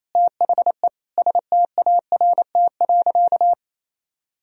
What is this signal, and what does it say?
Morse code, heard as a keyed tone, about 28 wpm, 700 Hz. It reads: THE START.
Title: Sentences from Top 1000 Words - Review Part 1 - 28wpm